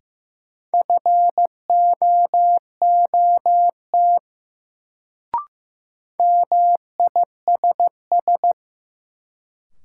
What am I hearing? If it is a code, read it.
Morse code, heard as a keyed tone, about 15 wpm, 700 Hz. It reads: FOOT MISS
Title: Sentences from Top 500 Words - New Word Drill - 15wpm